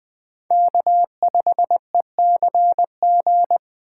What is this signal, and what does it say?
Morse code, heard as a keyed tone, about 20 wpm, 700 Hz. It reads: K5ECG
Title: USA Call Signs - 2x Character Spacing - 20wpm